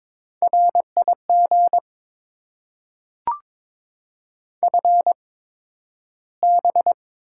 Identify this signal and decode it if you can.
Morse code, heard as a keyed tone, about 22 wpm, spaced as for 12 wpm, 700 Hz. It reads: RIG FB
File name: Top 20 QSO Elements - 8x Character Spacing - 22wpm